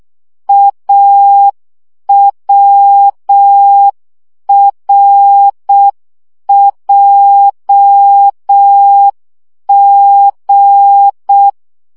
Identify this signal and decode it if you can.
Morse code, heard as a keyed tone, about 6 wpm, 800 Hz. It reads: AWRJG
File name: morze vizsga 2011-1 5